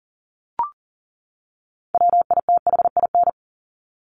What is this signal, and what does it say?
Morse code, heard as a keyed tone, about 40 wpm, 700 Hz. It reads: WITHIN
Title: Top 501-750 Words 40wpm